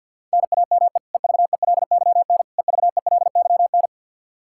Morse code, heard as a keyed tone, about 49 wpm, 700 Hz. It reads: NAME EVELYN EVELYN
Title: Practice QSO - Exchange 1 - Beginning - 50wpm